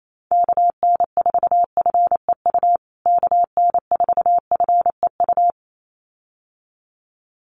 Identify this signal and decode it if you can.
Morse code, heard as a keyed tone, about 28 wpm, 700 Hz. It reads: KN4FEU KN4FEU
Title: USA Call Signs Repeated 2x 28wpm